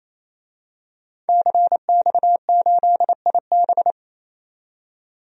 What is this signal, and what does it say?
Morse code, heard as a keyed tone, about 27 wpm, 700 Hz. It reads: CX8IB
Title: DXCC-100 Call Signs - 5 Characters - 28wpm